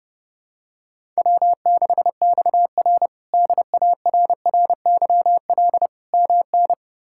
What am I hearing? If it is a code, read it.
Morse code, heard as a keyed tone, about 30 wpm, 700 Hz. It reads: W6XR DARRYL MN
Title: K1USN SST QSO Practice - 30wpm